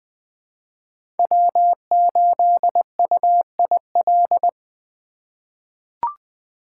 Morse code, heard as a keyed tone, about 20 wpm, 700 Hz. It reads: W8UIL
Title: K1USN SST QSO Practice - 20wpm